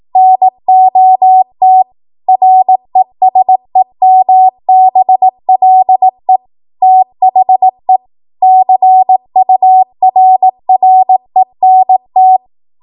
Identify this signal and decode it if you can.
Morse code, heard as a keyed tone, about 18 wpm, 750 Hz. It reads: NOT RESEMBLE THE CURRENT
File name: Morse / 18WPM